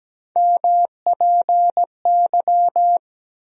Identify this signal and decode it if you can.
Morse code, heard as a keyed tone, about 17 wpm, 700 Hz. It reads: MPY
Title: USA Call Sign Postfixes - 3 Letters 17wpm